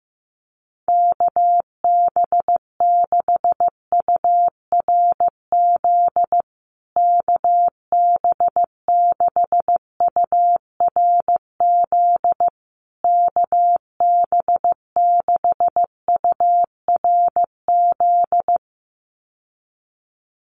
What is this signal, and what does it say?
Morse code, heard as a keyed tone, about 15 wpm, 700 Hz. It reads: KB6URZ KB6URZ KB6URZ
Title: USA Call Signs Repeated 3x 15wpm